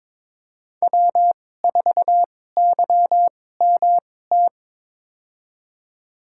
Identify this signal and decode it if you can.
Morse code, heard as a keyed tone, about 22 wpm, spaced as for 16 wpm, 700 Hz. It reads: W4YMT
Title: USA Call Signs - 5 Characters - 2x Character Spacing - 22wpm